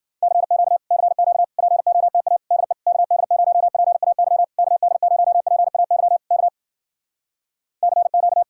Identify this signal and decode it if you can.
Morse code, heard as a keyed tone, about 52 wpm, 700 Hz. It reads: CQ CQ POTA DE KD0PAY KD0PAY K CQ